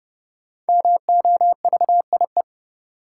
Morse code, heard as a keyed tone, about 30 wpm, 700 Hz. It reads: MOVIE
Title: Sentences from Top 1000 Words - Review Part 5 - New Word Drill - 30wpm